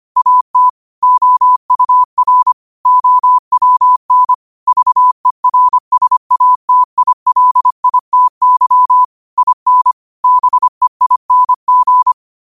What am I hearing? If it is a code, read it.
Morse code, heard as a keyed tone, about 25 wpm, 1000 Hz. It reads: AT OUR OWN VERSATILITY IN BEING